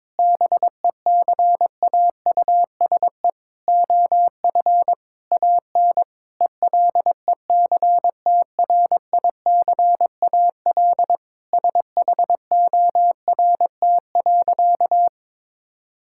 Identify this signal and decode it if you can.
Morse code, heard as a keyed tone, about 21 wpm, 700 Hz. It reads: BECAUSE OF AN ELECTRICAL SHORT.
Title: Sentences from Top 1000 Words - Review Part 6 - 22wpm